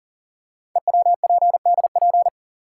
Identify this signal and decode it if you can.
Morse code, heard as a keyed tone, about 40 wpm, 700 Hz. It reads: EWPDP
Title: Code Groups - 5 Letters - 40wpm